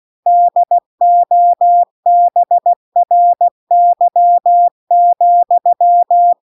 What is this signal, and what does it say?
Morse code, heard as a keyed tone, about 16 wpm, 700 Hz. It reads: DOBRY,